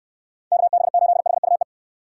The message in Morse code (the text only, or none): KB8HUE